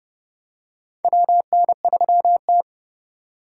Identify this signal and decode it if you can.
Morse code, heard as a keyed tone, about 30 wpm, 700 Hz. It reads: WN3T
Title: USA Call Signs - 4 Characters - 2x Character Spacing - 30wpm